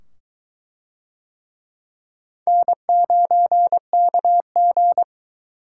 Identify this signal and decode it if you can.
Morse code, heard as a keyed tone, about 23 wpm, 700 Hz. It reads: N9KG